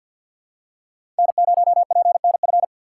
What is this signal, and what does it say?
Morse code, heard as a keyed tone, about 46 wpm, 690 Hz. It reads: N0PNF
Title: USA Call Signs - 5 Characters - 4x Character Spacing - 50wpm